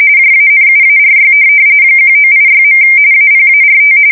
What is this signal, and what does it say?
RTTY, about 45.45 baud. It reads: JANUARY 1986. APART
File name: fsk170 / USS-Massachusetts